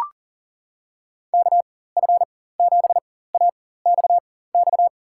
Morse code, heard as a keyed tone, about 37 wpm, spaced as for 18 wpm, 700 Hz. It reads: KF7AXX
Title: USA Call Signs - 6 Characters - 4x Character Spacing - 40wpm